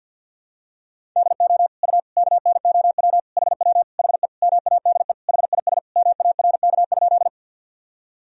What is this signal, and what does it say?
Morse code, heard as a keyed tone, about 47 wpm, 700 Hz. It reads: DO U KNOW HW HE MADE HIS MARK?
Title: Sentences from Top 300 Words and Abbreviations 50wpm